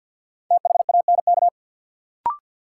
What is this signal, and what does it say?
Morse code, heard as a keyed tone, about 53 wpm, 700 Hz. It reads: THANK
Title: Sentences from Top 600 Words - New Word Drill - 50wpm